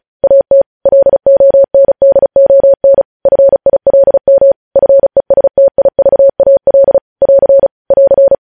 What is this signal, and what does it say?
Morse code, heard as a keyed tone, about 36 wpm, 550 Hz. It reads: AT LONDON FILM FESTIVAL <AR> <AR>